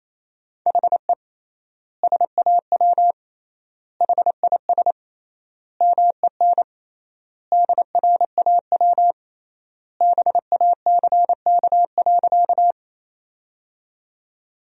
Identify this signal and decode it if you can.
Morse code, heard as a keyed tone, about 28 wpm, 700 Hz. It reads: HE SAW HIS MEN DRAW BACK.
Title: 3x Word-Spacing - Sentences from Top 300 Words - 28wpm